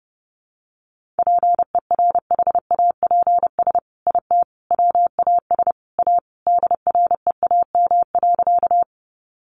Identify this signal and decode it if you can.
Morse code, heard as a keyed tone, about 30 wpm, 700 Hz. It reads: PERHAPS IT WAS A DREAM.
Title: Sentences from Top 1500 Words 30wpm